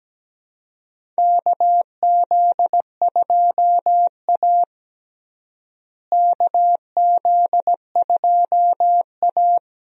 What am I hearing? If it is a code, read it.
Morse code, heard as a keyed tone, about 17 wpm, 700 Hz. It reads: KZ2A KZ2A
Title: USA Call Signs - 4 Characters - Repeated 2x - 17wpm